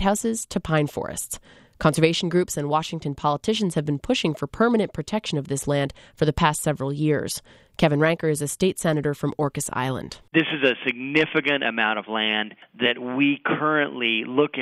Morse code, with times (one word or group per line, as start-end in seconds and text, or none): none